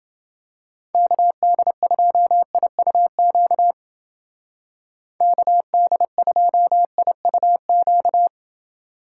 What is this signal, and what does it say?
Morse code, heard as a keyed tone, about 29 wpm, 700 Hz. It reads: KD2IUQ KD2IUQ